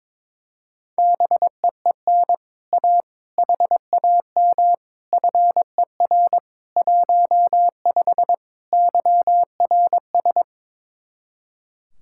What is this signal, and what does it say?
Morse code, heard as a keyed tone, about 22 wpm, 700 Hz. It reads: BEEN A HAM FER 15 YRS
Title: Speed-Racing Common QSO Phrases - 22wpm